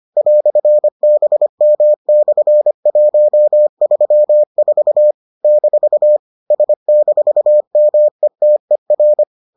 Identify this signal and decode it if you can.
Morse code, heard as a keyed tone, about 25 wpm, 600 Hz. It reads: "BM/134 - S-METER